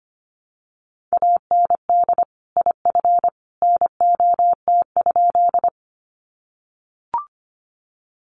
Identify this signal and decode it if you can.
Morse code, heard as a keyed tone, about 25 wpm, 700 Hz. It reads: AND IF NOT?
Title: Sentences from Top 600 Words 25wpm